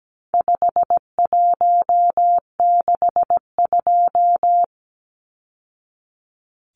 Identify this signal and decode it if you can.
Morse code, heard as a keyed tone, about 17 wpm, 700 Hz. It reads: 5162
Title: Numbers - 4-Digit 17wpm